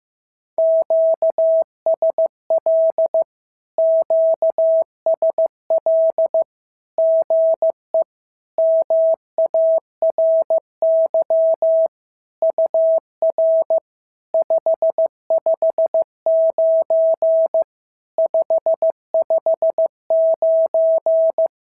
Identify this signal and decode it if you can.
Morse code, heard as a keyed tone, about 15 wpm, 650 Hz. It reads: QSL QSL GE MARY UR 559 559
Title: SOTA QSO Practice 15wpm